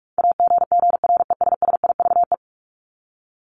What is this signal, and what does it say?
Morse code, heard as a keyed tone, about 44 wpm, 700 Hz. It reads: AGGRESSIVE
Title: Top 1501-1750 Words 45wpm